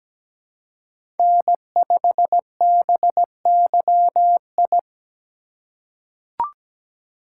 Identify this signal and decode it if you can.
Morse code, heard as a keyed tone, about 17 wpm, 700 Hz. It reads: N5BYI
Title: USA Call Signs - 3x Character Spacing - 17wpm